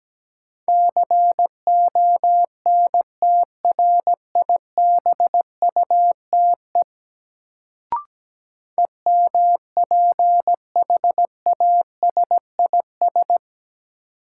Morse code, heard as a keyed tone, about 17 wpm, 700 Hz. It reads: CONTRIBUTE EMPHASIS
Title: Sentences from Top 1400 Words - New Word Drill - 17wpm